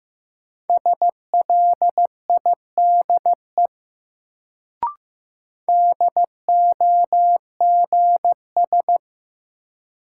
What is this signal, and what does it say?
Morse code, heard as a keyed tone, about 15 wpm, 700 Hz. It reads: SLIDE DOGS
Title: Sentences from Top 1600 Words - New Word Drill Alt - 15wpm